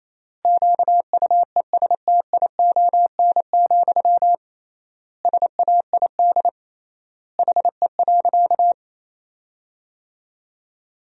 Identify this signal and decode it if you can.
Morse code, heard as a keyed tone, about 28 wpm, 700 Hz. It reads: QUESTION, SAID HE.
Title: 3x Word-Spacing - Sentences from Top 400 Words - 28wpm